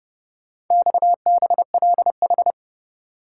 Morse code, heard as a keyed tone, about 28 wpm, 700 Hz. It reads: XBLH